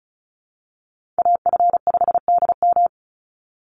Morse code, heard as a keyed tone, about 34 wpm, 700 Hz. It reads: AF5DM
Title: Call Signs Repeated 2x - 35wpm